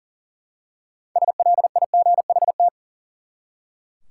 Morse code, heard as a keyed tone, about 40 wpm, 700 Hz. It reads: SLIGHT
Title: Six Letter Words - 40wpm